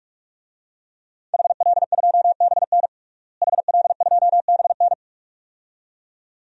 Morse code, heard as a keyed tone, about 42 wpm, 700 Hz. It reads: HL2BN HL2BN
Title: DXCC-100 Call Signs - 5 Characters - Repeated 2x - 45wpm